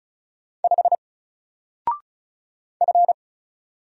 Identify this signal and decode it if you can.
Morse code, heard as a keyed tone, about 35 wpm, 700 Hz. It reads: 5 F
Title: Single Character (Letter:Number:Punctuation:Prosign) 35wpm